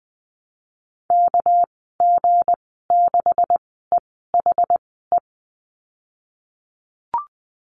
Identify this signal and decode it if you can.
Morse code, heard as a keyed tone, about 20 wpm, spaced as for 15 wpm, 700 Hz. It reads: KG6EHE
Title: 207 Lesson 33g - Letter j - Call Signs 20wpm